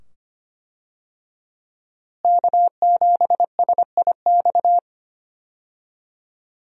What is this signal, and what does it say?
Morse code, heard as a keyed tone, about 25 wpm, 700 Hz. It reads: K7SIX
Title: USA Call Signs - 5 Characters - Repeated 2x - 25wpm